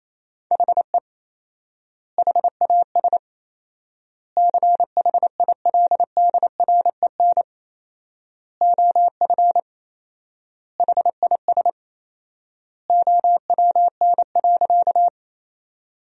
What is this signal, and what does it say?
Morse code, heard as a keyed tone, about 28 wpm, 700 Hz. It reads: HE HAS CHILDREN OF HIS OWN.
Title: 4x Word-Spacing - Sentences from Top 300 Words - 28wpm